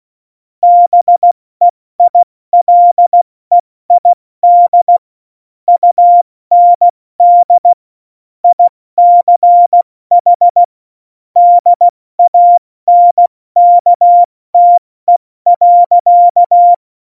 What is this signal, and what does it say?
Morse code, heard as a keyed tone, about 16 wpm, 700 Hz. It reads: BEILEID UND ICH DANKTE.
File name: Morsekurs / Witze / Witz 21 Juli 2020, 70 F80